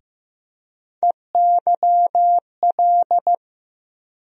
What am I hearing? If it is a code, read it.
Morse code, heard as a keyed tone, about 15 wpm, 700 Hz. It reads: EYL
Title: USA Call Sign Postfixes - 3 Letters 15wpm